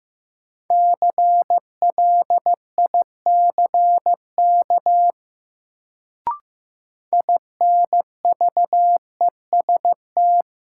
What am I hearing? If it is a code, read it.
Morse code, heard as a keyed tone, about 15 wpm, 700 Hz. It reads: CLICK INVEST
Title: Sentences from Top 1100 Words - New Word Drill Alt - 15wpm